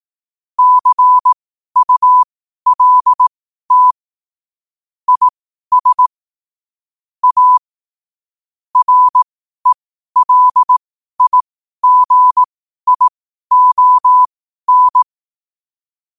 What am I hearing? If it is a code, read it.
Morse code, heard as a keyed tone, about 18 wpm, spaced as for 12 wpm, 1000 Hz. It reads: CULT IS A RELIGION
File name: qotdfiles / 12wpm